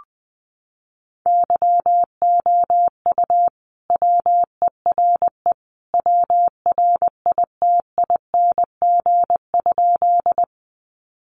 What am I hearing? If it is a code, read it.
Morse code, heard as a keyed tone, about 20 wpm, 700 Hz. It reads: YOU WERE WRITING?
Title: Sentences from Top 800 Words 20wpm